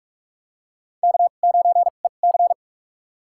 Morse code, heard as a keyed tone, about 47 wpm, spaced as for 33 wpm, 700 Hz. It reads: K9EC
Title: USA Call Signs - 4 Characters - 2x Character Spacing - 45wpm